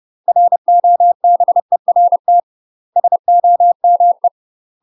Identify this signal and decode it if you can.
Morse code, heard as a keyed tone, about 30 wpm, 700 Hz. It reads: ROBERT SOME
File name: UK MASTER QSOs TRAINING / 30WPM / UK MASTER QSOs 0030